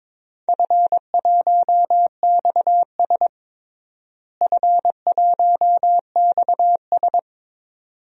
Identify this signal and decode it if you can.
Morse code, heard as a keyed tone, about 22 wpm, 700 Hz. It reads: F1XS F1XS
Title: DXCC-100 Call Signs - 4 Characters - Repeated 2x - 22wpm